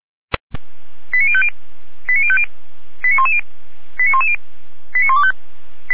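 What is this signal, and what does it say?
Five-tone selective call (ZVEI), heard as quick runs of five tones, tones of about 70 ms. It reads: 89050 89050 89110 89110 89125
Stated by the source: ZVEI five-tone call 89250 received by SDR on channel 455, logged by feuerwehr-orlamuende fire brigade server.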